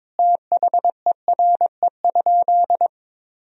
Morse code, heard as a keyed tone, about 22 wpm, 700 Hz. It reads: THERE?
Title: Sentences from Top 100 Words and Abbreviations 22wpm